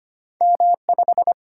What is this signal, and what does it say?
Morse code, heard as a keyed tone, about 25 wpm, 690 Hz. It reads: M5
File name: Two Letter-Number Combinations - 25wpm